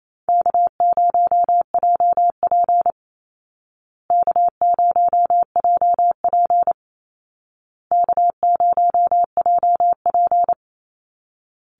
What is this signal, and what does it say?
Morse code, heard as a keyed tone, about 28 wpm, 700 Hz. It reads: K0JP K0JP K0JP
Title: Call Signs Repeated 2x - 28wpm